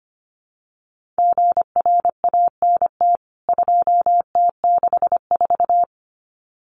Morse code, heard as a keyed tone, about 25 wpm, 700 Hz. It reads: GRANT 2T64
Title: CWT Exchange 25wpm